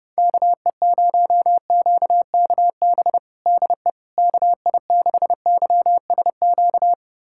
Morse code, heard as a keyed tone, about 28 wpm, 700 Hz. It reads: KE0QKB DE KI6YSQ
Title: USA Callsign DE Callsign - 6 Characters - 30wpm